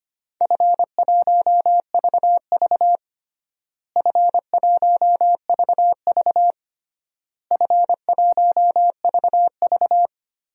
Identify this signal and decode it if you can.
Morse code, heard as a keyed tone, about 25 wpm, 700 Hz. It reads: F1VV F1VV F1VV
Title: DXCC-100 Call Signs - 4 Characters - Repeated 3x - 25wpm